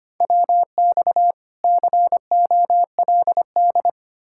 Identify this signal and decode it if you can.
Morse code, heard as a keyed tone, about 33 wpm, 700 Hz. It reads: WX COLD